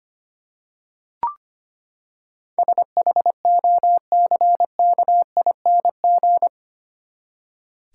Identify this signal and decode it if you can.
Morse code, heard as a keyed tone, about 25 wpm, 700 Hz. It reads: SHOCKING
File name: Eight Letter Words - 25wpm